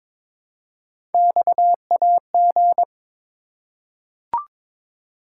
Morse code, heard as a keyed tone, about 23 wpm, 700 Hz. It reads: XAG
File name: practice / 22wpm / Three Letter-Number Combinations - 22wpm